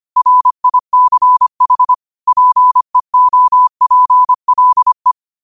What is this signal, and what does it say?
Morse code, heard as a keyed tone, about 25 wpm, 1000 Hz. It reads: RICH PEOPLE
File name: qotdfiles / 25wpm